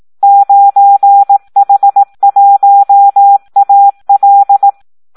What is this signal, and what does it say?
Morse code, heard as a keyed tone, about 18 wpm, 800 Hz. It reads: 9H1AL
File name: Audio practica CW / SD 1 18